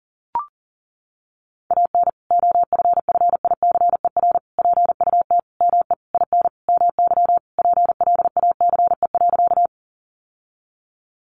Morse code, heard as a keyed tone, about 40 wpm, 700 Hz. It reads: AN OFFICER PUT ME IN MY PLACE.
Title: Sentences from Top 1200 Words 40wpm